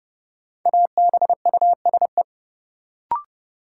Morse code, heard as a keyed tone, about 32 wpm, 700 Hz. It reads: ABUSE
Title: Sentences from Top 1300 Words - New Word Drill - 30wpm